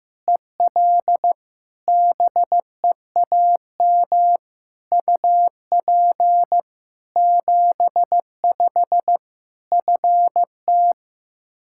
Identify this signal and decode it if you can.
Morse code, heard as a keyed tone, about 16 wpm, 700 Hz. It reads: EL BEAM UP 75 FT
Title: K1USN SST QSO Practice 15wpm